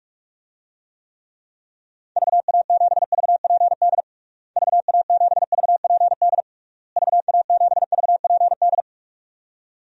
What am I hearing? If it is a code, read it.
Morse code, heard as a keyed tone, about 46 wpm, 700 Hz. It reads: VA7VPD VA7VPD VA7VPD
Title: DXCC-100 Call Signs - 6 Characters - Repeated 3x - 45wpm